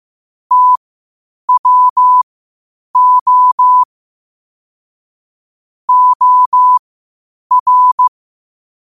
Morse code, heard as a keyed tone, about 15 wpm, spaced as for 8 wpm, 1000 Hz. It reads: TWO OR
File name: 8wpm